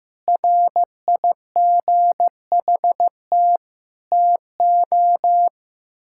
Morse code, heard as a keyed tone, about 15 wpm, 700 Hz. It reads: RIGHT TO
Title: Sentences from Top 1000 Words - Review Part 6 - 15wpm